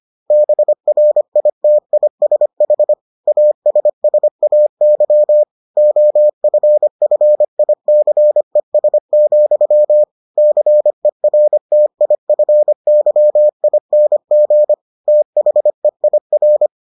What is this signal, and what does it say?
Morse code, heard as a keyed tone, about 25 wpm, 600 Hz. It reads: BRITISH ASSAY OFFICES, CERTIFYING THEIR